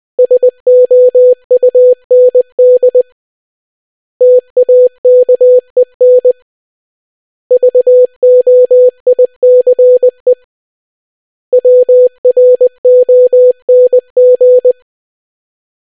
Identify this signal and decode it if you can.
Morse code, heard as a keyed tone, about 20 wpm, 500 Hz. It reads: SOUND TAKEN VOICE WRONG